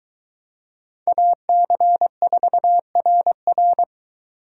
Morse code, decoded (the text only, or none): AC4RR